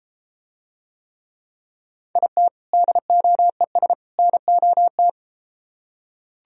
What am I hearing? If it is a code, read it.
Morse code, heard as a keyed tone, about 30 wpm, 700 Hz. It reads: IT DOES NOT